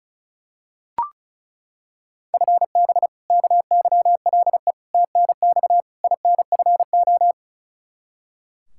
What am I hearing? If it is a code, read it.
Morse code, heard as a keyed tone, about 34 wpm, 700 Hz. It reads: FB KYLE TNX INFO
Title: Practice QSO - Exchange 3 - More Info - 35wpm